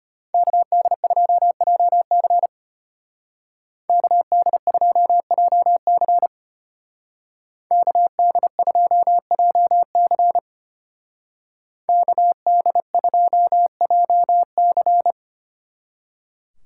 Morse code, heard as a keyed tone, about 31 wpm, 700 Hz. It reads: KD2JC KD2JC KD2JC KD2JC
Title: Speed Racing - USA 2x2 Call Signs 25wpm